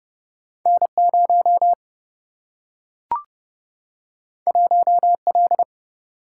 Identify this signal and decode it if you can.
Morse code, heard as a keyed tone, about 30 wpm, 700 Hz. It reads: N 0 1 L
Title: Single-Letter Single-Number 30wpm